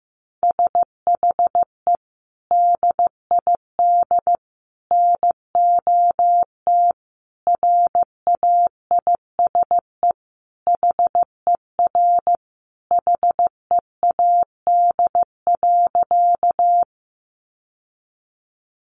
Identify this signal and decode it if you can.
Morse code, heard as a keyed tone, about 15 wpm, 700 Hz. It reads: SHE DID NOT RAISE HER HEAD.
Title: Sentences from Top 1400 Words 15wpm